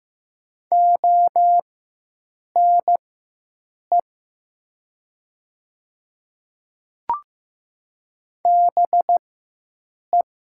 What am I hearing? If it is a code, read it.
Morse code, heard as a keyed tone, about 15 wpm, spaced as for 7 wpm, 700 Hz. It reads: ONE BE
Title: 4x Character Spacing - Top 300 Words - 15wpm